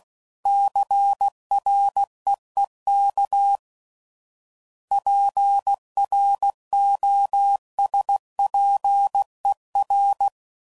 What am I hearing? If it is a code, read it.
Morse code, heard as a keyed tone, about 16 wpm, 800 Hz. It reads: CREEK PROSPER
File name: CW Code Groups @15WPM / fromSM's / Cities